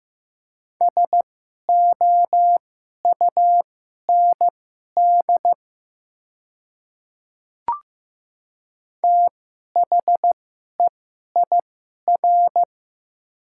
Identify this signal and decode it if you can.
Morse code, heard as a keyed tone, about 15 wpm, spaced as for 11 wpm, 700 Hz. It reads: SOUND THEIR
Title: 2x Character Spacing - Top 300 Words - 15wpm